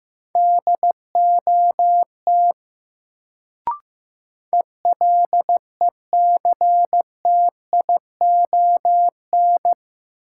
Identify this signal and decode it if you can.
Morse code, heard as a keyed tone, about 15 wpm, 700 Hz. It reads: DOT ELECTION
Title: Sentences from Top 1500 Words - New Word Drill Alt - 15wpm